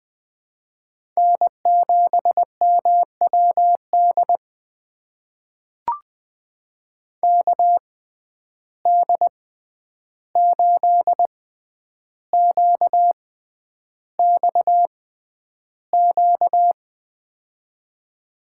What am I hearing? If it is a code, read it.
Morse code, heard as a keyed tone, about 20 wpm, spaced as for 10 wpm, 700 Hz. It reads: N7MWD KD8QXQ
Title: USA Call Signs - 6x Character Spacing - 20wpm